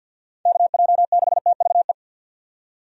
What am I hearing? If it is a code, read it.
Morse code, heard as a keyed tone, about 51 wpm, 700 Hz. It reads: KJ6TVE
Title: USA Call Signs - 6 Characters - 50wpm